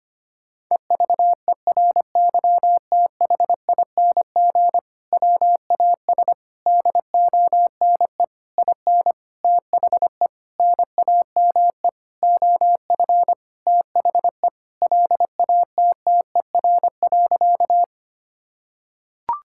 Morse code, heard as a keyed tone, about 25 wpm, 700 Hz. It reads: EVERYTHING WAS DONE IN THE NAME OF THE LATTER.
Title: Sentences from Top 1600 Words 25wpm